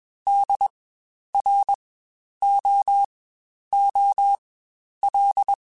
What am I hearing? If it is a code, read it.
Morse code, heard as a keyed tone, about 21 wpm, spaced as for 10 wpm, 800 Hz. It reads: DROOL
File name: Spaced Alphas / Spaced Alphas mp3s / Ds sp @20